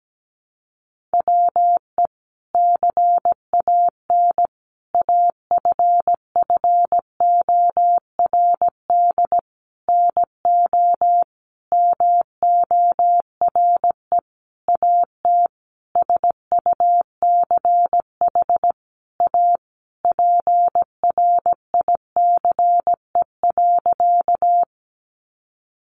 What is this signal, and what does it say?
Morse code, heard as a keyed tone, about 17 wpm, 700 Hz. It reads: WE CAN AFFORD NO MORE AT SUCH A PRICE.
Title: Sentences from Top 1100 Words 17wpm